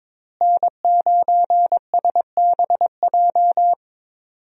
Morse code, heard as a keyed tone, about 22 wpm, 700 Hz. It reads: N9SBJ
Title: USA Call Signs - 5 Characters - Repeated 3x - 22wpm